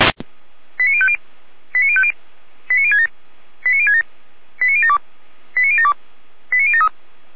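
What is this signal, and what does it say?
Five-tone selective call (ZVEI), heard as quick runs of five tones, tones of about 70 ms. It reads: 89050 89050 89067 89067 89072 89072 89073
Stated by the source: ZVEI five-tone selective call 89025 received with an SDR.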